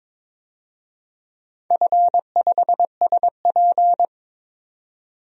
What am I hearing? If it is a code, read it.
Morse code, heard as a keyed tone, about 22 wpm, 700 Hz. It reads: F5SP
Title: DXCC-100 Call Signs - 4 Characters - 22wpm